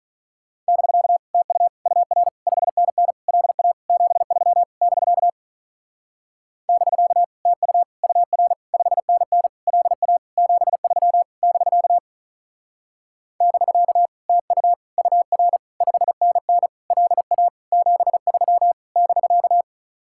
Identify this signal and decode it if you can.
Morse code, heard as a keyed tone, about 42 wpm, 700 Hz. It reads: <BK> TU UR 5NN LA 73 <BK> <BK> TU UR 5NN LA 73 <BK> <BK> TU UR 5NN LA 73 <BK>
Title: Speed-Racing Common QSO Phrases - 35wpm